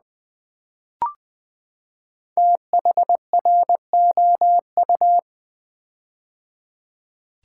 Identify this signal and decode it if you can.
Morse code, heard as a keyed tone, about 20 wpm, 700 Hz. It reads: THROU
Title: Build A Word - 5 Letters At A Time - 10-Letter Words - 20wpm